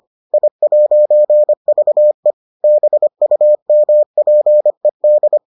Morse code, heard as a keyed tone, about 25 wpm, 600 Hz. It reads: I'VE BUMPED